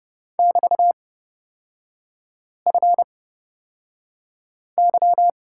Morse code, heard as a keyed tone, about 30 wpm, spaced as for 5 wpm, 700 Hz. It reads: <BT>FY
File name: Lesson 36e - BT - Three Characters 30wpm